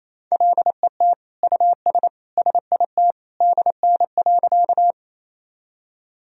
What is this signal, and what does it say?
Morse code, heard as a keyed tone, about 28 wpm, 700 Hz. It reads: LET US SIT DN.
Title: Sentences from Top 500 Words and Abbreviations 28wpm